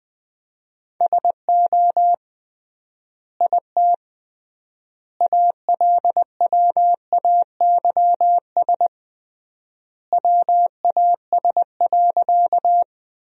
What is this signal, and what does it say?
Morse code, heard as a keyed tone, about 20 wpm, 700 Hz. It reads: SO IT ALWAYS WAS.
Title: 3x Word-Spacing - Sentences from Top 300 Words - 20wpm